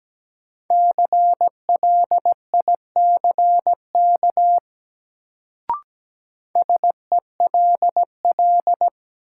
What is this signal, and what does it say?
Morse code, heard as a keyed tone, about 17 wpm, 700 Hz. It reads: CLICK SELL
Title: Sentences from Top 1100 Words - New Word Drill Alt - 17wpm